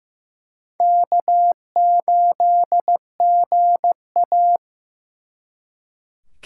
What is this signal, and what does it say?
Morse code, heard as a keyed tone, about 18 wpm, 700 Hz. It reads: K8GA
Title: Speed Racing - USA 1x2 Call Signs 15wpm